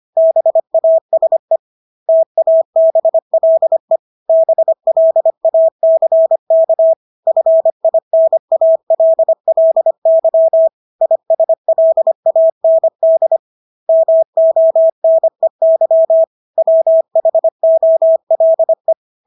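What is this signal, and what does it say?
Morse code, heard as a keyed tone, about 25 wpm, 650 Hz. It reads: BASE TABLE BLACK FINALLY ISLAND MONEY WHOLE
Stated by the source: Morse practice of English words at 25 wpm from CW med Gnister.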